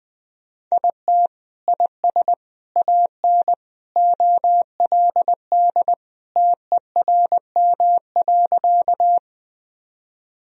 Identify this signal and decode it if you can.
Morse code, heard as a keyed tone, about 21 wpm, 700 Hz. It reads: IT IS AN OLD TERM.
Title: Sentences from Top 1000 Words - Review Part 4 - 20wpm